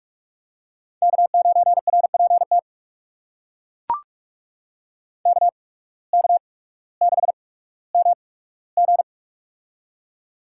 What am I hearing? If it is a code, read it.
Morse code, heard as a keyed tone, about 42 wpm, spaced as for 19 wpm, 700 Hz. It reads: K9RPT KK6MG
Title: USA Call Signs - 5 Characters - 8x Character Spacing - 45wpm